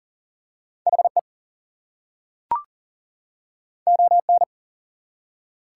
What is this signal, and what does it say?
Morse code, heard as a keyed tone, about 40 wpm, 700 Hz. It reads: HE ON
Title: Two Letter Words - 40wpm